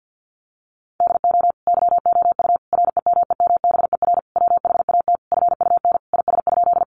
Morse code, heard as a keyed tone, about 50 wpm, 700 Hz. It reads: DO YOU REMEMBER WHAT FUN IS?
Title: Sentences from Top 700 Words 50wpm